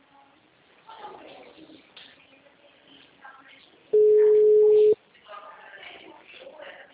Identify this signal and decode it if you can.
Ringback tone (Europe, 425 Hz), with speech.